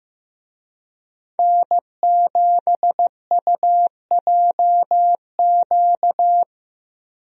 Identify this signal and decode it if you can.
Morse code, heard as a keyed tone, about 15 wpm, 700 Hz. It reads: N7UJQ
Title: USA Call Signs - 5 Characters - 15wpm